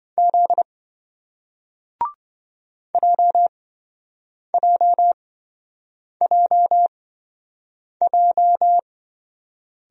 Morse code, heard as a keyed tone, about 28 wpm, spaced as for 7 wpm, 700 Hz. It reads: Z JJJJ